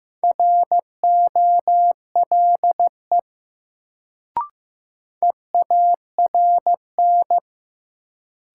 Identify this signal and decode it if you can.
Morse code, heard as a keyed tone, about 15 wpm, 700 Hz. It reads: ROLE EARN
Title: Sentences from Top 1500 Words - New Word Drill Alt - 15wpm